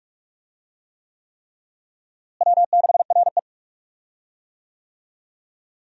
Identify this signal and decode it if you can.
Morse code, heard as a keyed tone, about 46 wpm, 700 Hz. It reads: W6RE
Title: USA Call Signs - 4 Characters - Repeated 3x - 45wpm